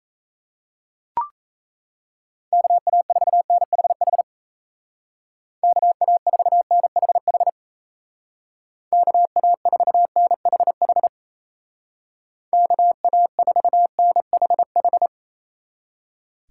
Morse code, heard as a keyed tone, about 35 wpm, 700 Hz. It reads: KA4NHH KA4NHH KA4NHH KA4NHH